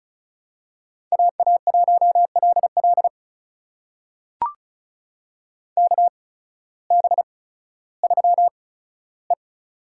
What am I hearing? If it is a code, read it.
Morse code, heard as a keyed tone, about 34 wpm, spaced as for 16 wpm, 700 Hz. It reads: AA1LL KB3E
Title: USA Call Signs - 8x Character Spacing - 35wpm